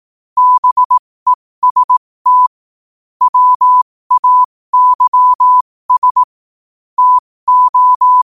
Morse code, heard as a keyed tone, about 18 wpm, spaced as for 15 wpm, 1000 Hz. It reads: BEST WAYS TO
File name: qotdfiles / 15wpm